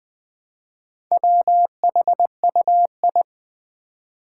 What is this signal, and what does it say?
Morse code, heard as a keyed tone, about 21 wpm, 700 Hz. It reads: WHUI
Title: Four Challenging Letter-Number Combinations 20wpm